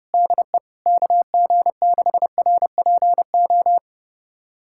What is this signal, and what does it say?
Morse code, heard as a keyed tone, about 30 wpm, 700 Hz. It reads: DE KG6RPO
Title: USA Callsign DE Callsign 30wpm